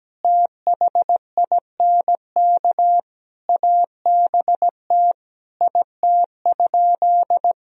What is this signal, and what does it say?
Morse code, heard as a keyed tone, about 17 wpm, 700 Hz. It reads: THINK ABT IT?
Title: Sentences from Top 200 Words and Abbreviations 17wpm